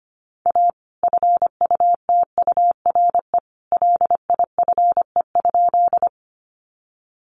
Morse code, heard as a keyed tone, about 25 wpm, 700 Hz. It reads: A FUTURE LIFE?
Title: Sentences from Top 700 Words 25wpm